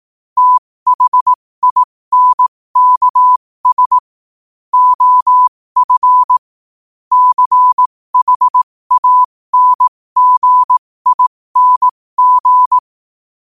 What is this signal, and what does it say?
Morse code, heard as a keyed tone, about 18 wpm, spaced as for 15 wpm, 1000 Hz. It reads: THINKS OF CHANGING
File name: qotdfiles / 15wpm